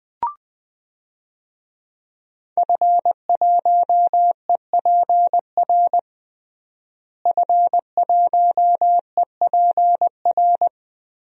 Morse code, heard as a keyed tone, about 21 wpm, 700 Hz. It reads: F1EPR F1EPR
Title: DXCC-100 Call Signs - 5 Characters - Repeated 3x - 20wpm